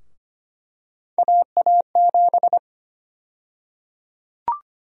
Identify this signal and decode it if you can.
Morse code, heard as a keyed tone, about 25 wpm, 700 Hz. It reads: AA7
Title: USA Call Sign Prefixes - 25wpm